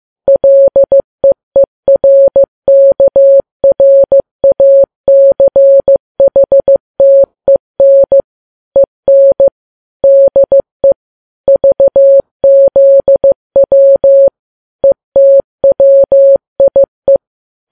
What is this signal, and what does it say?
Morse code, heard as a keyed tone, about 15 wpm, 550 Hz. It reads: LEERKRACHTEN EN DE VZW ETWIE
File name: Files Web tls / Morse on9fv / morse15kps